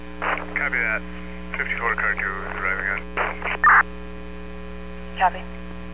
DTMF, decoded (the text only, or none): none